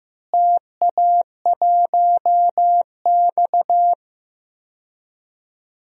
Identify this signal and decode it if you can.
Morse code, heard as a keyed tone, about 15 wpm, 700 Hz. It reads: TA1X
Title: DXCC-100 Call Signs - 4 Characters - Repeated 3x - 15wpm